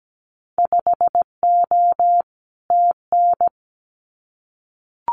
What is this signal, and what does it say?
Morse code, heard as a keyed tone, about 17 wpm, 700 Hz. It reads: 5O TN
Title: Winter Field Day Exchanges Encore 17wpm